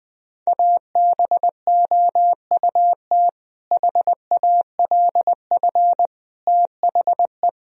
Morse code, heard as a keyed tone, about 20 wpm, 700 Hz. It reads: ABOUT HALF THE
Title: Sentences from Top 1000 Words - Review Part 1 - 20wpm